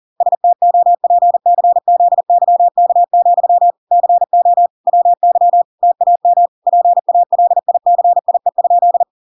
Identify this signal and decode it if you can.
Morse code, heard as a keyed tone, about 40 wpm, 700 Hz. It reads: STOPCZYK, CO WY TAM PALICIE?